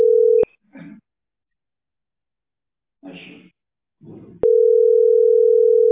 Ringback tone (North America), 440 + 480 Hz.